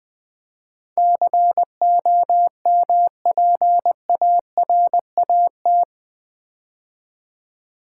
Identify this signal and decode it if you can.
Morse code, heard as a keyed tone, about 20 wpm, 700 Hz. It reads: COMPARAT
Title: Build A Word - 4 Letters At A Time - 11-Letter Words - 20wpm